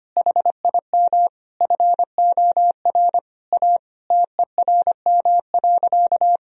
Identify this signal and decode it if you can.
Morse code, heard as a keyed tone, about 25 wpm, 700 Hz. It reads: HIM FOR A TERM.
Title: Sentences from Top 1000 Words - Review Part 4 - 25wpm